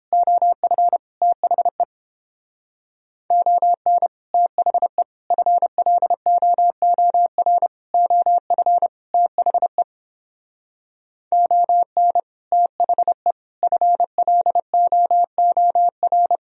Morse code, heard as a keyed tone, about 28 wpm, 700 Hz. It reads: OF THE ON THE FLOOR OF THE ON THE FLOOR